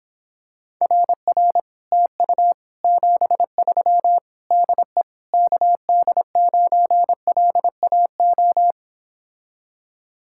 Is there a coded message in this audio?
Yes: Morse code from 0.8 to 8.7 s, heard as a keyed tone, about 26 wpm, 700 Hz. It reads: RR TU 73 DE KD9LAO